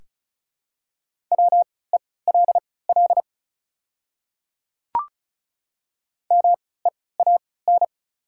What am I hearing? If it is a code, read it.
Morse code, heard as a keyed tone, about 34 wpm, spaced as for 20 wpm, 690 Hz. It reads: WELL MEAN
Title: 3x Character Spacing - Top 500 Words - 35wpm